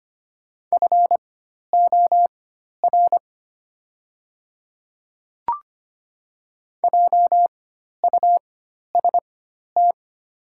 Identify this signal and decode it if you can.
Morse code, heard as a keyed tone, about 25 wpm, spaced as for 12 wpm, 700 Hz. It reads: FOR JUST
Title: 4x Character Spacing - Top 200 Words - 25wpm